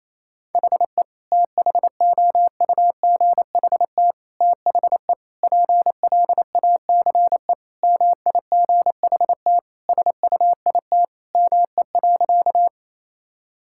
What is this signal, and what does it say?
Morse code, heard as a keyed tone, about 28 wpm, 700 Hz. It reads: HE THOUGHT THE PLACE MIGHT SUIT ME.
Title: Sentences from Top 1600 Words 28wpm